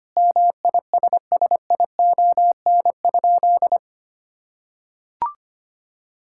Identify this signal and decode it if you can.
Morse code, heard as a keyed tone, about 25 wpm, 700 Hz. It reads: MISSION?
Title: Sentences from Top 1600 Words 25wpm